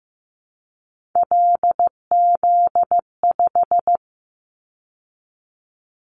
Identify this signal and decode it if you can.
Morse code, heard as a keyed tone, about 15 wpm, 700 Hz. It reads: LZ5
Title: Callsign Prefixes for Top 50 DXCC - 15wpm